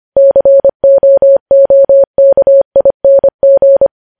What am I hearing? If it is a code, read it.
Morse code, heard as a keyed tone, about 25 wpm, 550 Hz. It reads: COOKING
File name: podcast / News headlines-1x-25x25